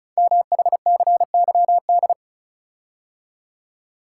Morse code, heard as a keyed tone, about 36 wpm, 700 Hz. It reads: MHCYD
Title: Code Groups - 5 Letters - 35wpm